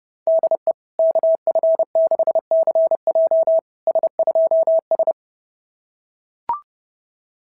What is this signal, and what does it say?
Morse code, heard as a keyed tone, about 30 wpm, 650 Hz. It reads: DE KF6CJ S2S
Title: SOTA QSO Practice 30wpm